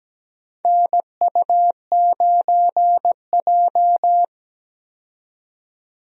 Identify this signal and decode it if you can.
Morse code, heard as a keyed tone, about 17 wpm, 700 Hz. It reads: NU9J
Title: USA Call Signs - 4 Characters - 17wpm